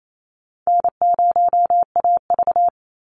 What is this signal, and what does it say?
Morse code, heard as a keyed tone, about 28 wpm, 700 Hz. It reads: N0AV
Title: Easy Call Signs - 28wpm